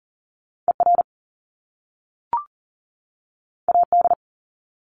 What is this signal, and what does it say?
Morse code, heard as a keyed tone, about 40 wpm, 700 Hz. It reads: ER AD
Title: Top Two Letter Combinations in Words 40wpm